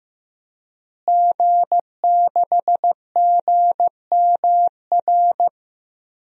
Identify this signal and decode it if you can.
Morse code, heard as a keyed tone, about 16 wpm, 700 Hz. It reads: G6GMR
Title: DXCC-100 Call Signs - 5 Characters - Repeated 2x - 15wpm